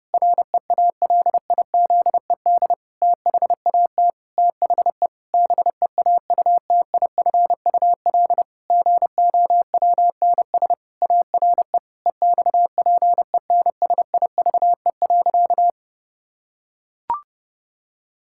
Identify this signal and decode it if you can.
Morse code, heard as a keyed tone, about 32 wpm, 700 Hz. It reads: REALIZED THAT THE BEAUTIFUL GOWNS ARE EXPENSIVE.
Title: Pangram Challenge v2 - 30wpm